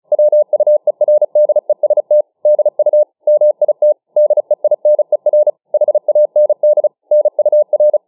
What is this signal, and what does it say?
Morse code, heard as a keyed tone, about 34 wpm, 600 Hz. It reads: WUERDEST DU MIT DEINER HAND NUR
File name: dk5ke / Morsemogeln 35WpM